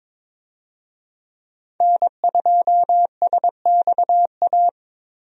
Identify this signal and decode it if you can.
Morse code, heard as a keyed tone, about 21 wpm, 700 Hz. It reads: N2SXA
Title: USA Call Signs - 5 Characters - Repeated 2x - 22wpm